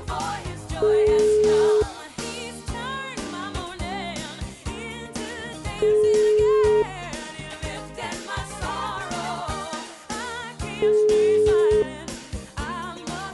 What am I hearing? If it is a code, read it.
Ringback tone (Europe, 425 Hz), with noise.